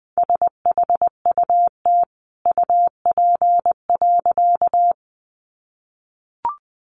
Morse code, heard as a keyed tone, about 20 wpm, 700 Hz. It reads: SHUT UP.
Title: A Sound Of Thunder 20wpm